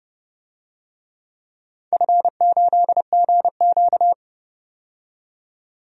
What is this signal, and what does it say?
Morse code, heard as a keyed tone, about 30 wpm, 700 Hz. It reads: F8GQ
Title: DXCC-100 Call Signs - 4 Characters - 30wpm